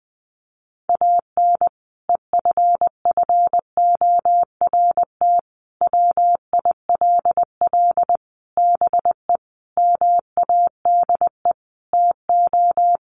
Morse code, heard as a keyed tone, about 20 wpm, 700 Hz. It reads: AN EFFORT WILL BE MADE TO
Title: Sentences from Top 1500 Words 20wpm